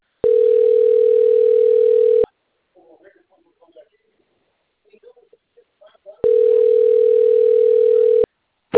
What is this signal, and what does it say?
Ringback tone (North America), 440 + 480 Hz.
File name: out- - - - - .663